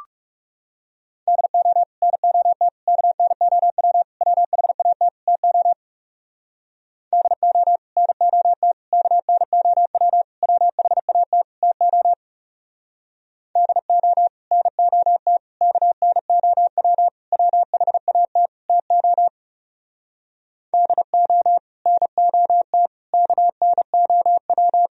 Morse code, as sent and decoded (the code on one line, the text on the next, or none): -.. --- / -. --- - / -.- -. --- .-- / .-- .... .- - / - --- / -.. --- / -. --- - / -.- -. --- .-- / .-- .... .- - / - --- / -.. --- / -. --- - / -.- -. --- .-- / .-- .... .- - / - --- / -.. --- / -. --- - / -.- -. --- .--
DO NOT KNOW WHAT TO DO NOT KNOW WHAT TO DO NOT KNOW WHAT TO DO NOT KNOW